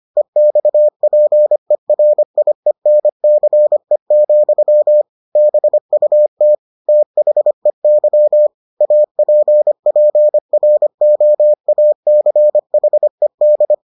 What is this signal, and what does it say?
Morse code, heard as a keyed tone, about 25 wpm, 600 Hz. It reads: EXPERIENCE, BUT THEY APPROACHED